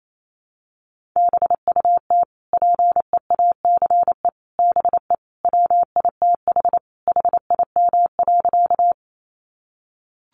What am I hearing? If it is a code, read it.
Morse code, heard as a keyed tone, about 28 wpm, 700 Hz. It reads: BUT PEACE BE WITH HIM.
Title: Sentences from Top 1500 Words 28wpm